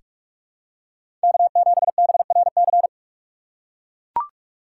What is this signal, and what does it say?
Morse code, heard as a keyed tone, about 44 wpm, 700 Hz. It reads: K7BRC
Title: USA Call Signs - 5 Characters - 6x Character Spacing - 45wpm